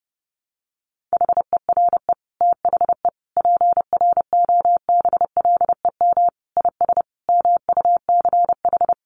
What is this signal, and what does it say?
Morse code, heard as a keyed tone, about 30 wpm, 700 Hz. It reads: HERE THE PROBLEM IS MUCH
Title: Sentences from Top 1200 Words 30wpm